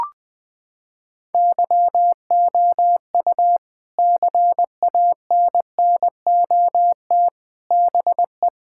Morse code, heard as a keyed tone, about 20 wpm, 700 Hz. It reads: YOU CANNOT BE
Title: Sentences from Top 1000 Words - Review Part 6 - 20wpm